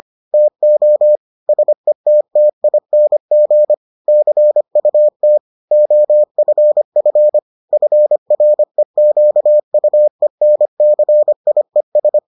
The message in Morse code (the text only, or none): TO SETTING CUT OFF FREQUENCIES